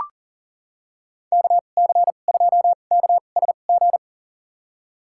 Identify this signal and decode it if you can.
Morse code, heard as a keyed tone, about 40 wpm, spaced as for 29 wpm, 700 Hz. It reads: KC2KSG
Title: USA Call Signs - 6 Characters - 2x Character Spacing - 40wpm